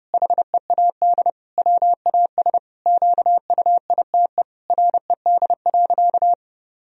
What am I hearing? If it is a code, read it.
Morse code, heard as a keyed tone, about 32 wpm, 700 Hz. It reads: HEAD WAS QUITE RED.
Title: Sentences from Top 1000 Words - Review Part 2 - 30wpm